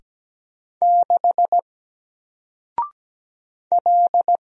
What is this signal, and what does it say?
Morse code, heard as a keyed tone, about 17 wpm, 700 Hz. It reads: 6 L
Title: Single Letter-Number Alt 17wpm